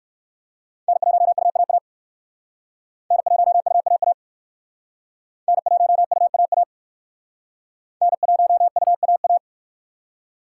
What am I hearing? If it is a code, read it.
Morse code, heard as a keyed tone, about 56 wpm, 690 Hz. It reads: N1UAA N1UAA N1UAA N1UAA
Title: Speed Racing - USA 1x3 Call Signs 45wpm